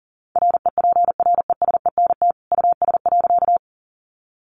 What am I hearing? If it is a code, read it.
Morse code, heard as a keyed tone, about 40 wpm, 700 Hz. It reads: REPRESENT US.
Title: Sentences from Top 1200 Words 40wpm